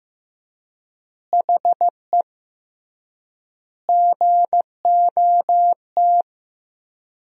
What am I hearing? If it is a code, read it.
Morse code, heard as a keyed tone, about 15 wpm, 700 Hz. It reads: HE GOT
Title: 3x Word-Spacing - Sentences from Top 300 Words - 15wpm